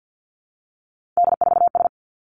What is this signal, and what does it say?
Morse code, heard as a keyed tone, about 50 wpm, 700 Hz. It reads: D4S
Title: Three Challenging Letter-Number Combinations 50wpm